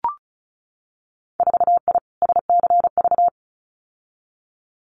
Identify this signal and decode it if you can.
Morse code, heard as a keyed tone, about 34 wpm, 700 Hz. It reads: 4I SCV
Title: Winter Field Day Exchanges Encore 35wpm